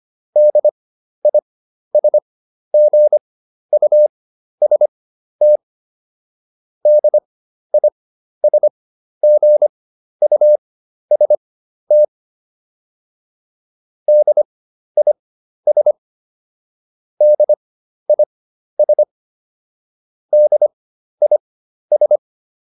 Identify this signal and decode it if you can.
Morse code, heard as a keyed tone, about 25 wpm, spaced as for 12 wpm, 600 Hz. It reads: DISGUST DISGUST DIS DIS DIS